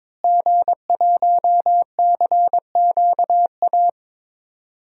Morse code, heard as a keyed tone, about 22 wpm, 700 Hz. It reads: G1CQA
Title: DXCC-100 Call Signs - 5 Characters - Repeated 3x - 22wpm